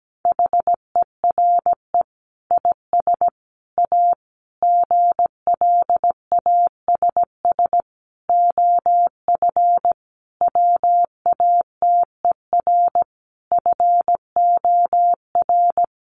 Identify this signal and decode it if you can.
Morse code, heard as a keyed tone, about 17 wpm, 700 Hz. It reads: HERE IS A GLASS OF WATER FOR
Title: Sentences from Top 1300 Words 17wpm